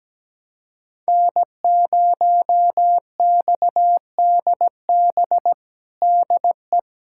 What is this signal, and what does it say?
Morse code, heard as a keyed tone, about 17 wpm, 700 Hz. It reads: N0XDB DE
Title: USA Callsign DE Callsign - 5 Characters - 17wpm